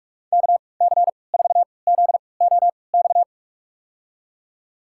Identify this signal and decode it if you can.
Morse code, heard as a keyed tone, about 44 wpm, spaced as for 25 wpm, 700 Hz. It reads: KC4ZOX
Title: USA Call Signs - 6 Characters - 3x Character Spacing - 45wpm